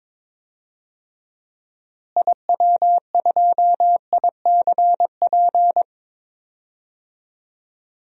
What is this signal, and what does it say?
Morse code, heard as a keyed tone, about 22 wpm, 700 Hz. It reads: IW2ICP
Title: DXCC-100 Call Signs - 6 Characters - Repeated 2x - 22wpm